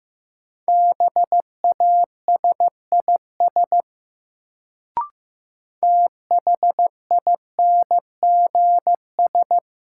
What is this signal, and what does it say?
Morse code, heard as a keyed tone, about 15 wpm, 700 Hz. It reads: BASIS THINGS
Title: Sets of 3 Words - New Word Drill - 15wpm